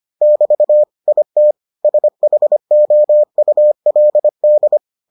Morse code, heard as a keyed tone, about 25 wpm, 600 Hz. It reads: <BT> IT SHOULD